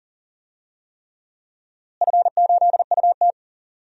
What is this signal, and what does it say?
Morse code, heard as a keyed tone, about 40 wpm, 700 Hz. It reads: F8UT
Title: DXCC-100 Call Signs - 4 Characters - Repeated 3x - 40wpm